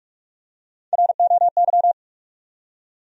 Morse code, heard as a keyed tone, about 46 wpm, 700 Hz. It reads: ROY
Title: USA Common Names - Max 4 Letters - 45wpm